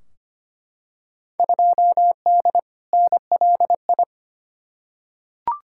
Morse code, heard as a keyed tone, about 25 wpm, 700 Hz. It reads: 2D NLI
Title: Field Day Exchanges Encore - 25wpm